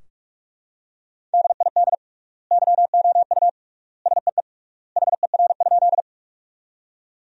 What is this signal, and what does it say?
Morse code, heard as a keyed tone, about 41 wpm, 700 Hz. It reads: DID YOU SEE HER?
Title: 3x Word-Spacing - Sentences from Top 100 Words - 45wpm